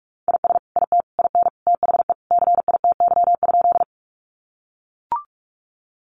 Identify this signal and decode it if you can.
Morse code, heard as a keyed tone, about 44 wpm, 700 Hz. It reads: IS IT IN THE CITY?
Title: Sentences from Top 300 Words 45wpm